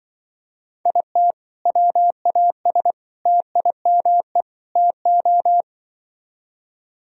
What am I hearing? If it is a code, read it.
Morse code, heard as a keyed tone, about 23 wpm, 700 Hz. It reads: IT WAS TIME TO
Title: Speed-Racing - Sets of 4 Words 20wpm